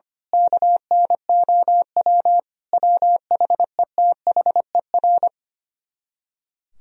Morse code, heard as a keyed tone, about 24 wpm, 700 Hz. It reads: KNOW WHETHER
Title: Speed-Racing - Sets of 5 Words 25wpm